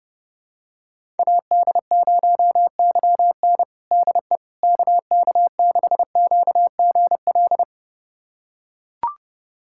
Morse code, heard as a keyed tone, about 31 wpm, 700 Hz. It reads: AD0YN DE KK6QGL
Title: USA Callsign DE Callsign 30wpm